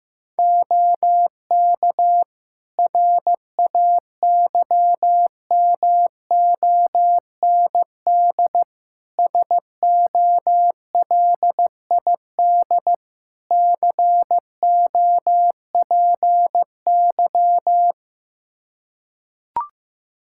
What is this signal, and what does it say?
Morse code, heard as a keyed tone, about 15 wpm, 700 Hz. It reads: OK RAYMOND SOLID COPY
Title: Practice QSO - Exchange 3 - More Info - 15wpm